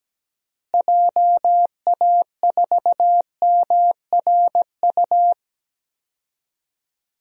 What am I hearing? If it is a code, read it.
Morse code, heard as a keyed tone, about 17 wpm, 700 Hz. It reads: JA4MRU
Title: DXCC-100 Call Signs - 6 Characters - Repeated 2x - 17wpm